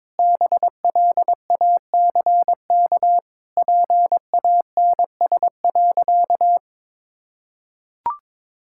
Morse code, heard as a keyed tone, about 22 wpm, 700 Hz. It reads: BLACK PANS.
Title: Pangram Challenge v2 - 22wpm